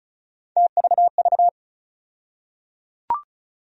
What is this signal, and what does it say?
Morse code, heard as a keyed tone, about 36 wpm, 700 Hz. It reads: TVV
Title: USA Call Sign Postfixes - 3 Letters 35wpm